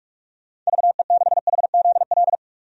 Morse code, heard as a keyed tone, about 43 wpm, 700 Hz. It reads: VE6HZL
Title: DXCC-100 Call Signs - 6 Characters - 45wpm